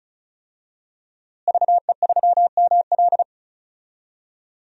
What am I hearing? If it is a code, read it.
Morse code, heard as a keyed tone, about 36 wpm, 700 Hz. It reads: VE3ML
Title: DXCC-100 Call Signs - 5 Characters - Repeated 3x - 35wpm